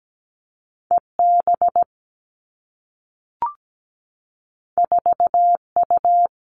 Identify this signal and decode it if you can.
Morse code, heard as a keyed tone, about 17 wpm, 700 Hz. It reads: EB 4U
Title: Two Challenging Letter-Number Combinations 17wpm